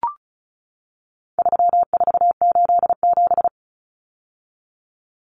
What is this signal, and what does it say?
Morse code, heard as a keyed tone, about 34 wpm, 700 Hz. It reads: 3487
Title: Numbers - 4-Digit 35wpm